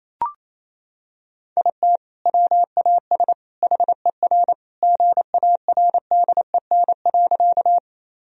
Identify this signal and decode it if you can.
Morse code, heard as a keyed tone, about 28 wpm, 700 Hz. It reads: IT WAS HER GARDEN.
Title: Sentences from Top 1000 Words - Review Part 4 - 28wpm